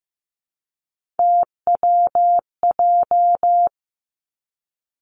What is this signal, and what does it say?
Morse code, heard as a keyed tone, about 15 wpm, 700 Hz. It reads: TWJ
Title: Three Challenging Letter-Number Combinations 15wpm